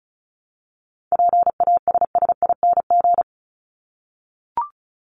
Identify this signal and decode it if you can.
Morse code, heard as a keyed tone, about 34 wpm, 700 Hz. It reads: PASSING
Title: ING Suffix - Top 100 Words with Base Words 35wpm